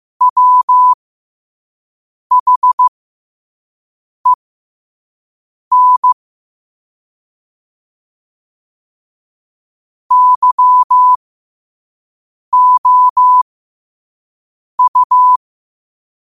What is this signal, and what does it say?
Morse code, heard as a keyed tone, about 15 wpm, spaced as for 5 wpm, 1000 Hz. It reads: WHEN YOU
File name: qotdfiles / 5wpm